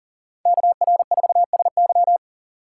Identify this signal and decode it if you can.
Morse code, heard as a keyed tone, about 48 wpm, 700 Hz. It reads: KR4SY
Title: Speed Racing - USA Call Signs 40wpm